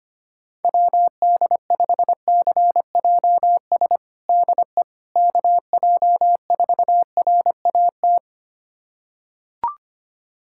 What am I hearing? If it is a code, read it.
Morse code, heard as a keyed tone, about 25 wpm, 700 Hz. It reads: WD5CJS DE KJ4RAT
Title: USA Callsign DE Callsign 25wpm